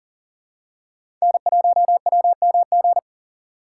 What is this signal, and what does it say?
Morse code, heard as a keyed tone, about 40 wpm, 700 Hz. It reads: N1WMG
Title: USA Call Signs - 4x Character Spacing - 40wpm